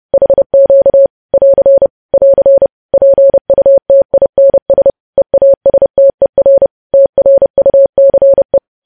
Morse code, heard as a keyed tone, about 30 wpm, 550 Hz. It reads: HQ <AR> <AR> PUTINS EASTER TRUCE
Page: news headlines in morse code at 30 wpm